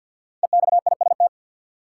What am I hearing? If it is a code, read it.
Morse code, heard as a keyed tone, about 53 wpm, 700 Hz. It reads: EXIST
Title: Five Letter Words - 50wpm